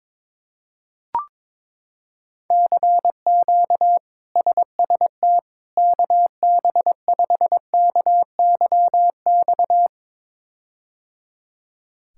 Morse code, heard as a keyed tone, about 22 wpm, 700 Hz. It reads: CQ SST KB5KYX
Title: K1USN SST QSO Practice - 22wpm